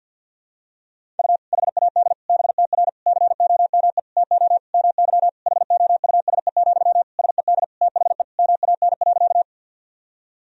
Morse code, heard as a keyed tone, about 52 wpm, 700 Hz. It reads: U HAD BTR COME TO MY HOUSE, SED THE MAN.
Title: Sentences from Top 400 Words and Abbreviations 50wpm